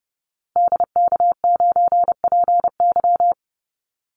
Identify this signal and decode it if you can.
Morse code, heard as a keyed tone, about 30 wpm, 700 Hz. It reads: DK9PY
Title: Call Signs Repeated 2x - 30wpm